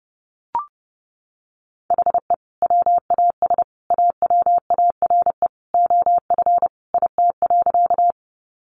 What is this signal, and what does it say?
Morse code, heard as a keyed tone, about 30 wpm, 700 Hz. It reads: HE WAS AWARE OF IT.
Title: Sentences from Top 1200 Words 30wpm